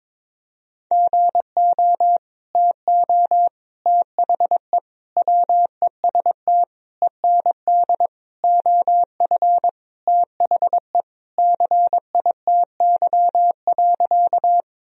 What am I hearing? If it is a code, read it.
Morse code, heard as a keyed tone, about 22 wpm, 700 Hz. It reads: GO TO THE WEST END OF THE CITY.
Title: Sentences from Top 1000 Words - Review Part 2 - 22wpm